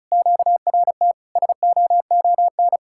Morse code, heard as a keyed tone, about 46 wpm, spaced as for 41 wpm, 700 Hz. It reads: QRT SOON
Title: Speed-Racing Common QSO Phrases Encore - 35wpm